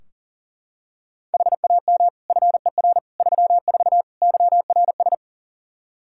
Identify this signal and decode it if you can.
Morse code, heard as a keyed tone, about 40 wpm, 700 Hz. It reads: HAM FER 34 YRS
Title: Practice QSO - Exchange 3 - More Info - 40wpm